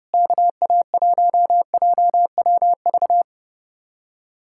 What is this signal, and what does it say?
Morse code, heard as a keyed tone, about 37 wpm, 700 Hz. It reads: KA1JWV
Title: Speed-Racing Common QSO Phrases - 30wpm